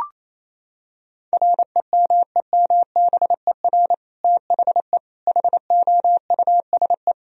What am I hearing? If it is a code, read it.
Morse code, heard as a keyed tone, about 28 wpm, 700 Hz. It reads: REMEMBER THE HOUSE
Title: Sentences from Top 1000 Words - Review Part 1 - 28wpm